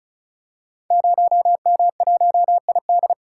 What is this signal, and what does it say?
Morse code, heard as a keyed tone, about 36 wpm, 700 Hz. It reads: 0M1ID
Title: Code Groups - 5 Letter:Numbers - 35wpm